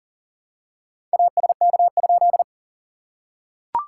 Morse code, heard as a keyed tone, about 40 wpm, 690 Hz. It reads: ASK?
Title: SOTA QSO Practice 40wpm